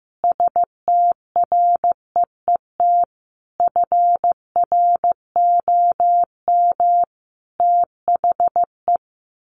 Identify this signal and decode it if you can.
Morse code, heard as a keyed tone, about 15 wpm, 700 Hz. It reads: STREET FROM THE
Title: Sets of 5 Words Encore 15wpm